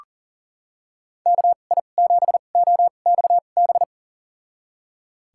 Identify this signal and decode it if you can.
Morse code, heard as a keyed tone, about 40 wpm, spaced as for 29 wpm, 700 Hz. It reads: KI7OXB
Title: USA Call Signs - 6 Characters - 2x Character Spacing - 40wpm